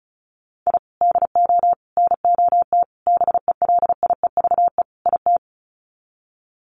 Morse code, heard as a keyed tone, about 34 wpm, 700 Hz. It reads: I DO NOT BELIEVE IT
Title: Sets of 5 Words 35wpm